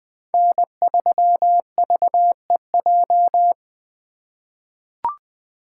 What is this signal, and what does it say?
Morse code, heard as a keyed tone, about 20 wpm, 700 Hz. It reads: N3VEJ
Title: 2023 POTA Activator Call Signs - 20wpm